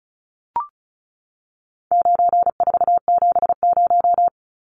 Morse code, heard as a keyed tone, about 34 wpm, 700 Hz. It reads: 9470
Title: Numbers - 4-Digit 35wpm